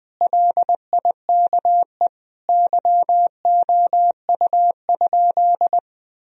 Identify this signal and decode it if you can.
Morse code, heard as a keyed tone, about 20 wpm, 700 Hz. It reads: LIKE YOU?
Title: Sentences from Top 1000 Words - Review Part 1 - 20wpm